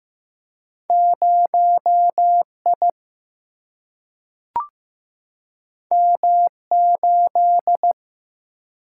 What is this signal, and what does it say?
Morse code, heard as a keyed tone, about 15 wpm, 700 Hz. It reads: 0 I M 8
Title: Single-Letter Single-Number 15wpm